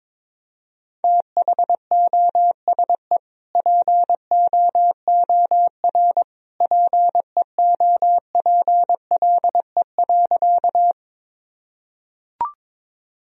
Morse code, heard as a keyed tone, about 22 wpm, 700 Hz. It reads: THOSE POOR PEOPLE.
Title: Sentences from Top 1000 Words - Review Part 1 - 22wpm